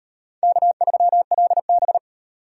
Morse code, heard as a keyed tone, about 38 wpm, 700 Hz. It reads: K3LB